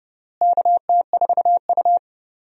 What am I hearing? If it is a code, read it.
Morse code, heard as a keyed tone, about 30 wpm, 700 Hz. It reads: KT4U
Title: USA Call Signs - 4 Characters - Repeated 3x - 30wpm